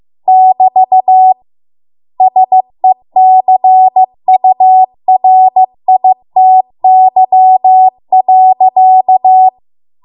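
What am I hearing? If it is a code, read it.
Morse code, heard as a keyed tone, about 15 wpm, 750 Hz. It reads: <BT> SECURITY.
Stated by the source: Morse code practice at 15 wpm from the national amateur radio association's code archive.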